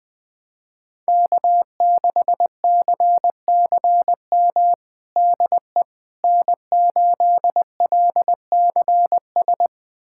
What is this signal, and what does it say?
Morse code, heard as a keyed tone, about 21 wpm, 700 Hz. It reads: K6CCM DE N8LCS
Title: USA Callsign DE Callsign - 5 Characters - 20wpm